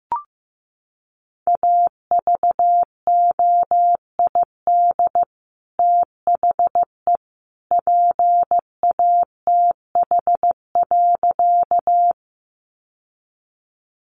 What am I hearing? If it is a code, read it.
Morse code, heard as a keyed tone, about 15 wpm, 700 Hz. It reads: AVOID THE PATH.
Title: Sentences from Top 1400 Words 15wpm